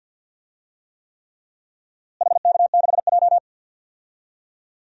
Morse code, heard as a keyed tone, about 52 wpm, 700 Hz. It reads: HK6J
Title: DXCC-100 Call Signs - 4 Characters - Repeated 3x - 50wpm